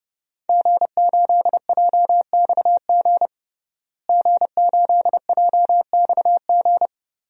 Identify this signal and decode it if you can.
Morse code, heard as a keyed tone, about 32 wpm, 700 Hz. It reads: G8JXG G8JXG
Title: DXCC-100 Call Signs - 5 Characters - Repeated 2x - 30wpm